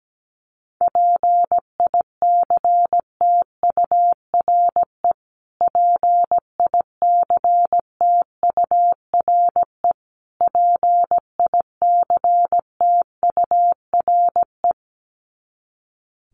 Morse code, heard as a keyed tone, about 17 wpm, 700 Hz. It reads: PICTURE PICTURE PICTURE
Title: Top 300 Words Repeated 17wpm